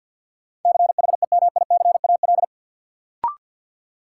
Morse code, heard as a keyed tone, about 53 wpm, 700 Hz. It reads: CHEMICAL
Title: Sentences from Top 1200 Words - New Word Drill Alt - 50wpm